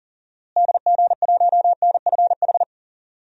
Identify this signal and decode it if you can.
Morse code, heard as a keyed tone, about 40 wpm, 700 Hz. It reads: DG1NFH
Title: DXCC-100 Call Signs - 6 Characters - Repeated 3x - 40wpm